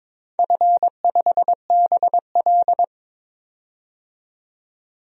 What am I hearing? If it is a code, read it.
Morse code, heard as a keyed tone, about 22 wpm, 700 Hz. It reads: F5BL
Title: DXCC-100 Call Signs - 4 Characters - 22wpm